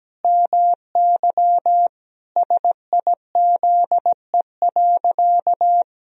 Morse code, heard as a keyed tone, about 17 wpm, 700 Hz. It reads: MY SIZE.
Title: Sentences from Top 1000 Words - Review Part 2 - 17wpm